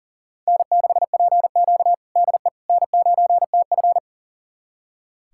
Morse code, heard as a keyed tone, about 40 wpm, 700 Hz. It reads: N6PQ DE N9TF
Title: USA Callsign DE Callsign - 4 Characters - 40wpm